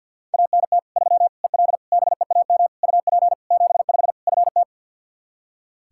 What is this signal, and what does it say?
Morse code, heard as a keyed tone, about 53 wpm, 700 Hz. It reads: ANT 3 EL BEAM UP 75 FT
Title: Practice QSO - Exchange 2 - Info - 50wpm